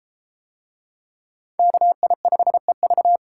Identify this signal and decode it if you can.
Morse code, heard as a keyed tone, about 30 wpm, 700 Hz. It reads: KI5EV